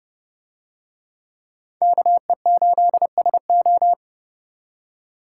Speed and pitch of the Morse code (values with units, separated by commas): 28 wpm, 700 Hz